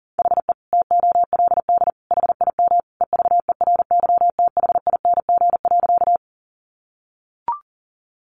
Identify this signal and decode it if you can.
Morse code, heard as a keyed tone, about 37 wpm, 700 Hz. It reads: HE TOLD HIM EVERYTHING.
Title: Sentences from Top 700 Words 40wpm